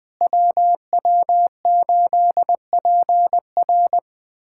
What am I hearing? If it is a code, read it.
Morse code, heard as a keyed tone, about 20 wpm, 700 Hz. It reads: WW8PR
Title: USA Callsign DE Callsign - 5 Characters - 20wpm